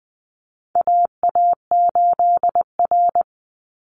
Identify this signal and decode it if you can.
Morse code, heard as a keyed tone, about 20 wpm, 700 Hz. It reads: AA8R
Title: Call Signs Repeated 3x - 20wpm